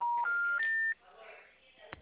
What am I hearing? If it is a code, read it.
Special information tone, with speech.
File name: out- -2016- - - .23889